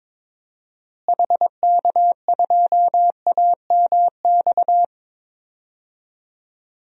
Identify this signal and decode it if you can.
Morse code, heard as a keyed tone, about 22 wpm, 700 Hz. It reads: HK2AMX
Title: DXCC-100 Call Signs - 6 Characters - Repeated 2x - 22wpm